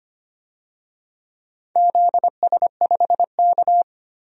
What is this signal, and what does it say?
Morse code, heard as a keyed tone, about 25 wpm, 700 Hz. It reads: ZS5K
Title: DXCC-100 Call Signs - 4 Characters - 25wpm